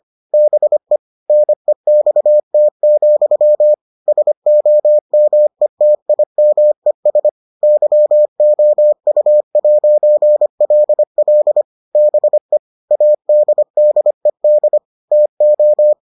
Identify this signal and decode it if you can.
Morse code, heard as a keyed tone, about 25 wpm, 600 Hz. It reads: BE NEXT, SOMETIMES YOU'LL BE ADDED TO